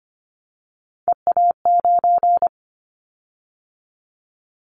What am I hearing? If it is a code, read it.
Morse code, heard as a keyed tone, about 25 wpm, 700 Hz. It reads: EA9
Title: Callsign Prefixes for Top 100 DXCC - 25wpm